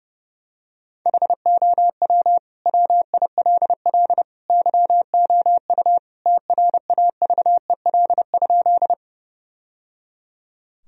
Morse code, heard as a keyed tone, about 30 wpm, 700 Hz. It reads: HOW WILL YOU TRAVEL?
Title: Sentences from Top 1000 Words - Review Part 2 - 30wpm